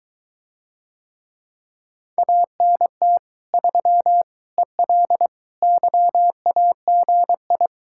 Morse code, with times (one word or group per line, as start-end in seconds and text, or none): 2.2-3.2 ANT
3.5-4.2 3
4.6-5.3 EL
5.6-7.7 YAGI